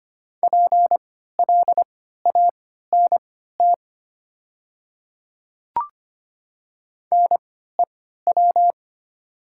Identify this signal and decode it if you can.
Morse code, heard as a keyed tone, about 25 wpm, spaced as for 14 wpm, 700 Hz. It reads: PLANT NEW
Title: 3x Character Spacing - Top 300 Words - 25wpm